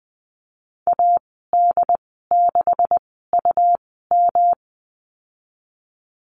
Morse code, heard as a keyed tone, about 20 wpm, spaced as for 14 wpm, 700 Hz. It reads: AD6UM
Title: 179 Lesson 29g - Slash - Call Signs 20wpm